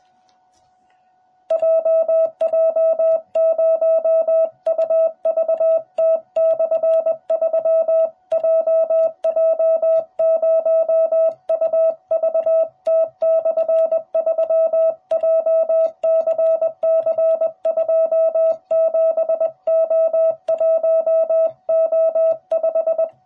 Morse code, heard as a keyed tone, about 21 wpm, 660 Hz. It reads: JJ0UVT/3 JJ0UVT/3 JCC27O1O5